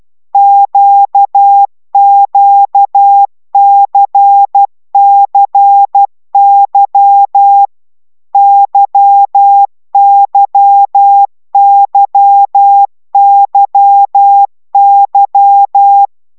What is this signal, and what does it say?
Morse code, heard as a keyed tone, about 12 wpm, 800 Hz. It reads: QQCCY YYYYY
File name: taviro tanulas / cw 75 15 10